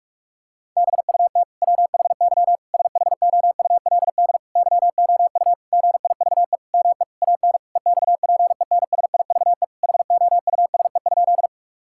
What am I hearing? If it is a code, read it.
Morse code, heard as a keyed tone, about 46 wpm, 700 Hz. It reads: BUT WHY SHOULD YOU GIVE ME AN EXPENSIVE HOUSE?
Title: Sentences from Top 1000 Words - Review Part 6 - 45wpm